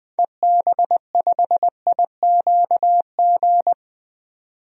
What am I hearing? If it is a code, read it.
Morse code, heard as a keyed tone, about 20 wpm, 700 Hz. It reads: EB5IQG
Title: DXCC-100 Call Signs - 6 Characters - 20wpm